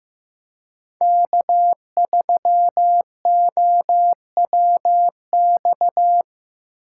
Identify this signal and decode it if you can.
Morse code, heard as a keyed tone, about 17 wpm, 700 Hz. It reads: K3OWX
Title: Speed Racing - USA 1x3 Call Signs 15wpm